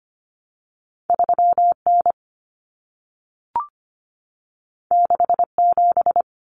Morse code, heard as a keyed tone, about 25 wpm, 700 Hz. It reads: 3N 67
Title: Common Cut Numbers - 2-Digit 25wpm